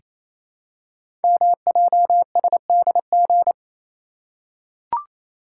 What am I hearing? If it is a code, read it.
Morse code, heard as a keyed tone, about 28 wpm, 700 Hz. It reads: MJSDG
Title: Code Groups - 5 Letters - 28wpm